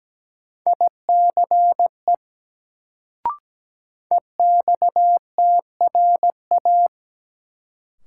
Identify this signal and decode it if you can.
Morse code, heard as a keyed tone, about 17 wpm, 700 Hz. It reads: ICE EXTRA
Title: Sentences from Top 1000 Words - New Word Drill - 17wpm